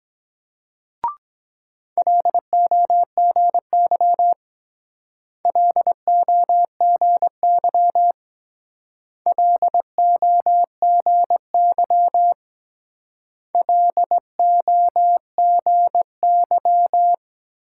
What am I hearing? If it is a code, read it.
Morse code, heard as a keyed tone, about 22 wpm, 700 Hz. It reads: LOGY LOGY LOGY LOGY